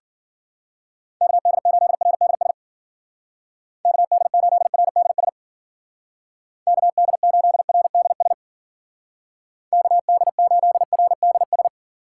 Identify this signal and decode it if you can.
Morse code, heard as a keyed tone, about 50 wpm, 700 Hz. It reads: KD8RDS KD8RDS KD8RDS KD8RDS
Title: Speed Racing - USA 2x3 Call Signs 40wpm